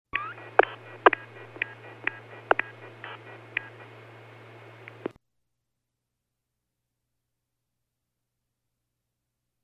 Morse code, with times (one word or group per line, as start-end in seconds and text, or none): none